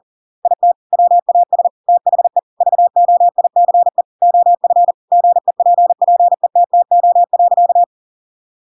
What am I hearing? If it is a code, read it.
Morse code, heard as a keyed tone, about 40 wpm, 700 Hz. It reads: IT WAS THE VOICE OF GEPPETTO.
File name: Pinocchio / Pinocchio ch6 40 40 ik1hge